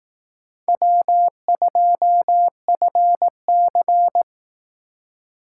Morse code, heard as a keyed tone, about 18 wpm, 700 Hz. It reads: W2FC
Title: Speed Racing - USA 1x2 Call Signs 15wpm